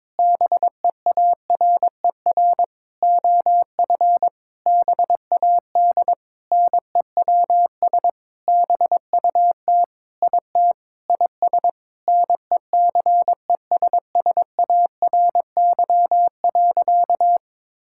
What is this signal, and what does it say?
Morse code, heard as a keyed tone, about 22 wpm, 700 Hz. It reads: BEARER OF BAD NEWS BUT IT IS NECESSARY.
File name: Sentences with 9-Word Phrases - 22wpm